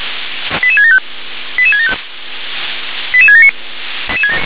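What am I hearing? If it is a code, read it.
Five-tone selective call (ZVEI), heard as quick runs of five tones, tones of about 80 ms. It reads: 88675 88675 88679
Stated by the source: ZVEI five-tone call 88620 received by SDR.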